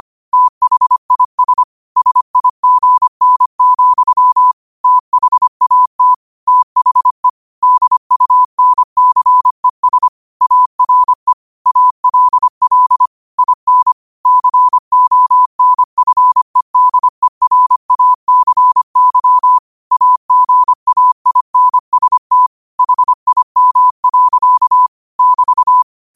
Morse code, heard as a keyed tone, about 25 wpm, 1000 Hz. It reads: THIS SIGN, THAT THE DUNCES ARE ALL IN CONFEDERACY AGAINST HIM. <BT>
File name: 25wpm